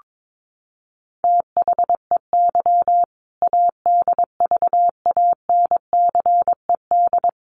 Morse code, heard as a keyed tone, about 22 wpm, 700 Hz. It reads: THEY ADVANCED
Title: Sentences from Top 1200 Words 22wpm